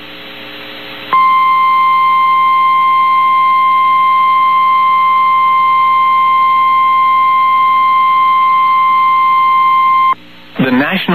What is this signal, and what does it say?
Emergency Alert System (SAME) tones (the 1050 Hz attention tone), with speech. It is heard from 1.1 to 10.2 s.
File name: eas received alerts / L1 19507 audio